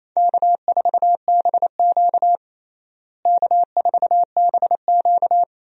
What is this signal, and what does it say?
Morse code, heard as a keyed tone, about 28 wpm, 700 Hz. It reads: K4BQ K4BQ
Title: USA Call Signs - 4 Characters - Repeated 2x - 28wpm